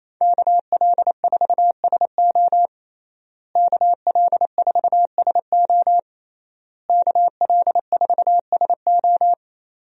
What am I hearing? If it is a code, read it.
Morse code, heard as a keyed tone, about 28 wpm, 700 Hz. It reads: KL4SO KL4SO KL4SO
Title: USA Call Signs - 5 Characters - Repeated 3x - 28wpm